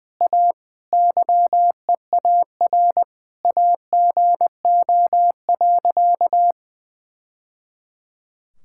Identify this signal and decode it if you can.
Morse code, heard as a keyed tone, about 20 wpm, 700 Hz. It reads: A YEAR AGO.
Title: Sentences from Top 1000 Words - Review Part 6 - 20wpm